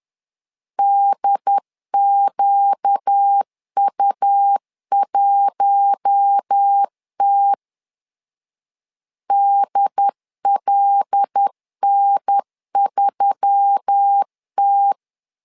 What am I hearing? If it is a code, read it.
Morse code, heard as a keyed tone, about 10 wpm, 800 Hz. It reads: DQU1T DLN3T
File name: LEZIONI / 5 LBVC / 5 L 10wpm 2259